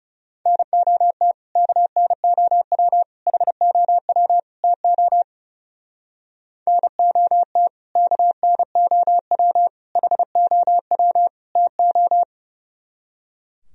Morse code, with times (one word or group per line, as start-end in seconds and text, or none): 0.5-1.3 NOT
1.6-3.0 KNOW
3.3-4.4 HOW
4.6-5.2 TO
6.7-7.7 NOT
8.0-9.7 KNOW
10.0-11.3 HOW
11.6-12.2 TO